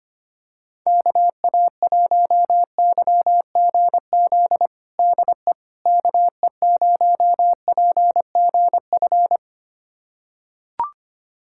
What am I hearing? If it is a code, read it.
Morse code, heard as a keyed tone, about 25 wpm, 700 Hz. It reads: KA1YGZ DE KE0PGF
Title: USA Callsign DE Callsign - 6 Characters - 25wpm